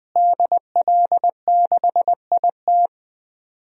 Morse code, heard as a keyed tone, about 20 wpm, 700 Hz. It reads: DL6IT
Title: DXCC-100 Call Signs - 5 Characters - Repeated 2x - 20wpm